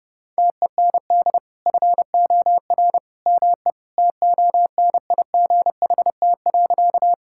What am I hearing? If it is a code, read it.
Morse code, heard as a keyed tone, about 32 wpm, 700 Hz. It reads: TEND FOR ME TONIGHT.
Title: Sentences from Top 1000 Words - Review Part 5 - 30wpm